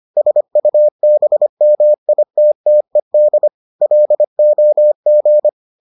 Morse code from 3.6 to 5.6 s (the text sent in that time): LOG